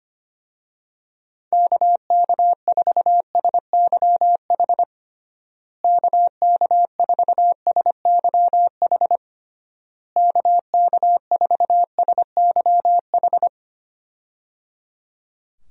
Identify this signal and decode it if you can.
Morse code, heard as a keyed tone, about 25 wpm, 700 Hz. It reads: KK4SYH KK4SYH KK4SYH
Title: USA Call Signs - 6 Characters - Repeated 3x - 25wpm